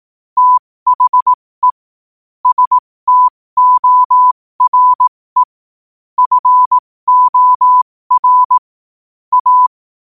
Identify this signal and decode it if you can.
Morse code, heard as a keyed tone, about 18 wpm, spaced as for 15 wpm, 1000 Hz. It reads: THE STORE FOR A
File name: qotdfiles / 15wpm